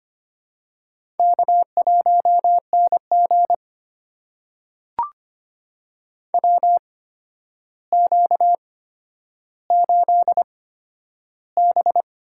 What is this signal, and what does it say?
Morse code, heard as a keyed tone, about 25 wpm, spaced as for 11 wpm, 700 Hz. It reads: K1NG WQ8B
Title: USA Call Signs - 4 Characters - 8x Character Spacing - 25wpm